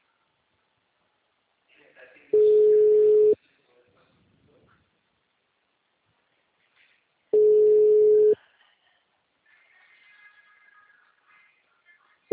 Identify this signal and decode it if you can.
Ringback tone (Europe, 425 Hz), with speech.
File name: out- -2046- - - .42379